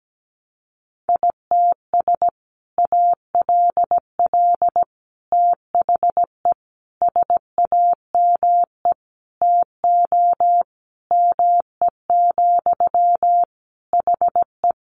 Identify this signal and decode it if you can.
Morse code, heard as a keyed tone, about 17 wpm, 700 Hz. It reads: ITS ALL THE SAME TO ME, HE
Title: Sentences from Top 600 Words 17wpm